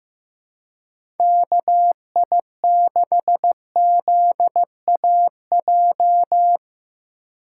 Morse code, heard as a keyed tone, about 16 wpm, 700 Hz. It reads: KI6ZAJ
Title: USA Call Signs - 6 Characters - 3x Character Spacing - 15wpm